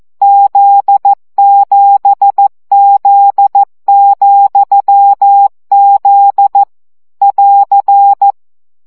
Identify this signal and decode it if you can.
Morse code, heard as a keyed tone, about 14 wpm, 800 Hz. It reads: Z7Z,Z <AR>
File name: CW gyakorlas z 7 vesszo 12 baud